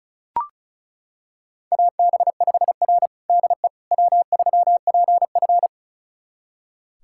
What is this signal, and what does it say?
Morse code, heard as a keyed tone, about 36 wpm, 700 Hz. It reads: AB5R DE W3PF
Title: USA Callsign DE Callsign - 4 Characters - 35wpm